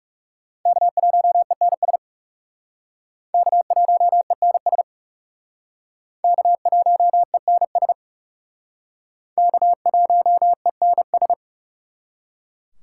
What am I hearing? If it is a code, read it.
Morse code, heard as a keyed tone, about 40 wpm, 700 Hz. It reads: K1ENS K1ENS K1ENS K1ENS